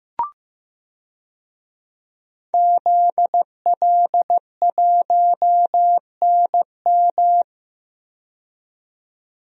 Morse code, heard as a keyed tone, about 15 wpm, 700 Hz. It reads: ZL1NM
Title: DXCC-100 Call Signs - 5 Characters - 15wpm